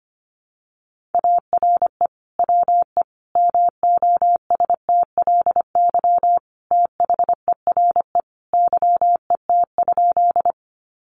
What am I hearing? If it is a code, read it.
Morse code, heard as a keyed tone, about 25 wpm, 700 Hz. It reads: ARE WE MOSTLY THERE YET?
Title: Sentences from Top 1300 Words 25wpm